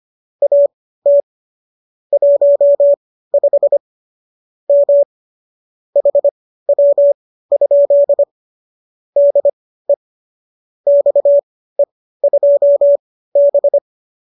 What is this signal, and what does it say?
Morse code, heard as a keyed tone, about 25 wpm, spaced as for 15 wpm, 570 Hz. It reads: AT 15 M HW? DE XE2B
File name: qso204 15